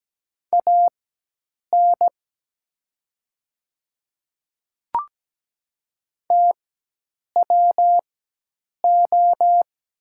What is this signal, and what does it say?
Morse code, heard as a keyed tone, about 17 wpm, spaced as for 8 wpm, 700 Hz. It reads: AN TWO
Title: 4x Character Spacing - Top 200 Words - 17wpm